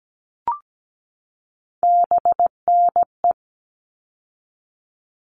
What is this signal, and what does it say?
Morse code, heard as a keyed tone, about 17 wpm, 700 Hz. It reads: BNE
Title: All Cut Numbers - 3-Digit 17wpm